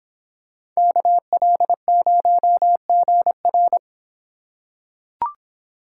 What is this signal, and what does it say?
Morse code, heard as a keyed tone, about 24 wpm, 700 Hz. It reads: KL0GR